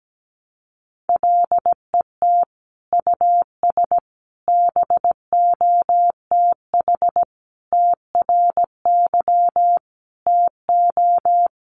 Morse code, heard as a keyed tone, about 17 wpm, 700 Hz. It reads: LET US BOTH TRY TO
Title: Sentences from Top 1200 Words 17wpm